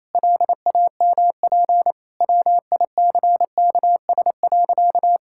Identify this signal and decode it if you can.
Morse code, heard as a keyed tone, about 28 wpm, 700 Hz. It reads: LAMP WICKS.
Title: Pangram Challenge v2 - 28wpm